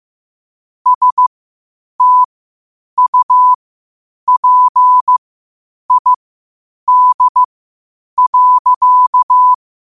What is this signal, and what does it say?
Morse code, heard as a keyed tone, about 15 wpm, spaced as for 8 wpm, 1000 Hz. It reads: STUPID.
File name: qotdfiles / 8wpm